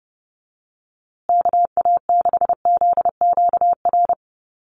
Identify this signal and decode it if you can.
Morse code, heard as a keyed tone, about 28 wpm, 700 Hz. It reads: KA6ZQR
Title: USA Call Signs Repeated 3x 30wpm